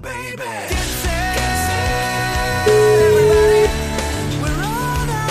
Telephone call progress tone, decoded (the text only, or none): ringback tone